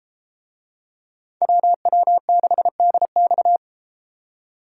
Morse code, heard as a keyed tone, about 31 wpm, 700 Hz. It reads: WW6DX